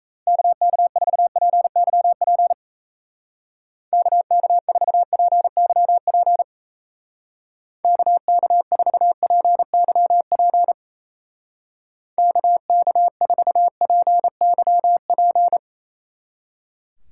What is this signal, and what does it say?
Morse code, heard as a keyed tone, about 36 wpm, 700 Hz. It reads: KK4PYP KK4PYP KK4PYP KK4PYP